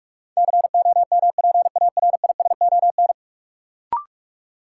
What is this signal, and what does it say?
Morse code, heard as a keyed tone, about 42 wpm, 700 Hz. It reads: COMPARISON
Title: Sentences from Top 1600 Words - New Word Drill Alt - 45wpm